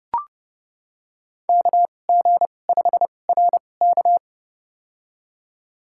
Morse code, heard as a keyed tone, about 30 wpm, spaced as for 22 wpm, 700 Hz. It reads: KG5RK
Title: USA Call Signs - 2x Character Spacing - 30wpm